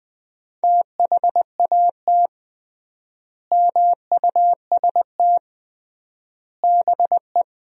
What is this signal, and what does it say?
Morse code, heard as a keyed tone, about 20 wpm, 700 Hz. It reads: THAT MUST BE
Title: 3x Word-Spacing - Sentences from Top 400 Words - 20wpm